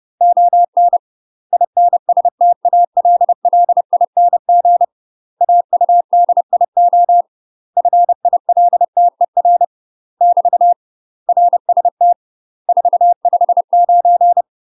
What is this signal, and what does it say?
Morse code, heard as a keyed tone, about 30 wpm, 700 Hz. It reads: ON INSTALLING AUDIO FILTER <BT> RST 459
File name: UK MASTER QSOs TRAINING / 30WPM / UK MASTER QSOs 0016